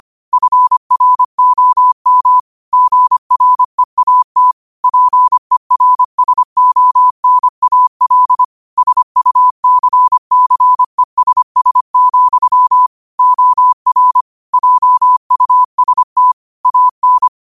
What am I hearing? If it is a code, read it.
Morse code, heard as a keyed tone, about 25 wpm, 1000 Hz. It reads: FROM GREAT PERSONAL SUCCESS, OR JUST AN